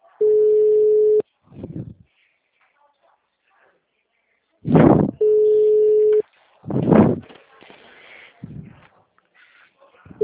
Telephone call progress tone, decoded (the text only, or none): ringback tone